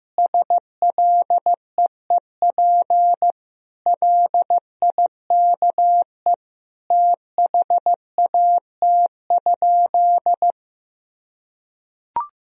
Morse code, heard as a keyed tone, about 15 wpm, 700 Hz. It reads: SLEEP LIKE THAT?
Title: Sentences from Top 1000 Words - Review Part 4 - 15wpm